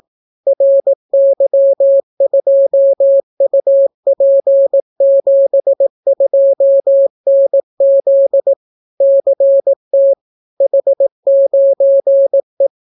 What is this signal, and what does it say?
Morse code, heard as a keyed tone, about 18 wpm, 550 Hz. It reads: RY2UP72NZ CT H9E